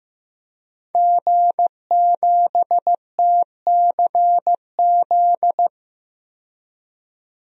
Morse code, heard as a keyed tone, about 15 wpm, 700 Hz. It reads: G7TCZ
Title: DXCC-100 Call Signs - 5 Characters - 15wpm